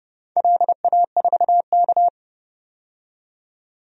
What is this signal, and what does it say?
Morse code, heard as a keyed tone, about 30 wpm, 700 Hz. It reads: LA4K
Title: DXCC-100 Call Signs - 4 Characters - 30wpm